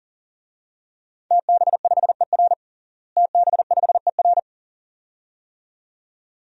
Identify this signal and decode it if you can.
Morse code, heard as a keyed tone, about 37 wpm, 700 Hz. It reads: TB5ER TB5ER
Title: DXCC-100 Call Signs - 5 Characters - Repeated 2x - 40wpm